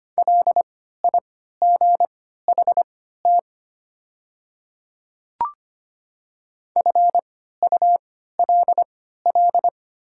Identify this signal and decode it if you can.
Morse code, heard as a keyed tone, about 25 wpm, spaced as for 14 wpm, 700 Hz. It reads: LIGHT FULL
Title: 3x Character Spacing - Top 500 Words - 25wpm